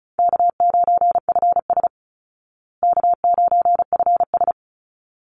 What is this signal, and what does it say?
Morse code, heard as a keyed tone, about 34 wpm, 700 Hz. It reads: K9FS K9FS
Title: Easy Call Signs Repeated 2x - 35wpm